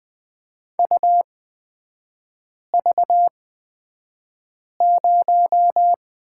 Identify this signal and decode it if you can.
Morse code, heard as a keyed tone, about 20 wpm, spaced as for 5 wpm, 700 Hz. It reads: UV0
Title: 250 Lesson 40e - BK - Three Characters 20wpm